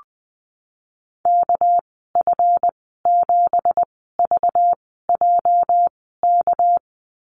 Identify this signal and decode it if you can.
Morse code, heard as a keyed tone, about 20 wpm, spaced as for 15 wpm, 700 Hz. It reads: KF7VJK
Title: 231 Lesson 37g - Letter x - Call Signs 20wpm